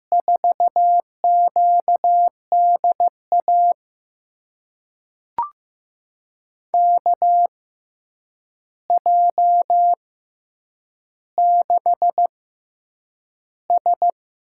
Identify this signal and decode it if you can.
Morse code, heard as a keyed tone, about 15 wpm, spaced as for 8 wpm, 700 Hz. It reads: 4QDA KJ6S
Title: USA Call Signs - 6 Characters - 6x Character Spacing - 15wpm